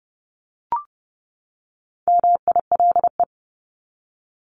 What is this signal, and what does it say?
Morse code, heard as a keyed tone, about 30 wpm, 700 Hz. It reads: MILE
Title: ES:S Suffix - Top 100 Words with Base Words 30wpm